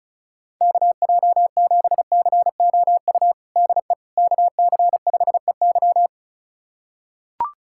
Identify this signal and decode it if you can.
Morse code, heard as a keyed tone, about 34 wpm, 700 Hz. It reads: KJ7COU DE KC5EY
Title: USA Callsign DE Callsign 35wpm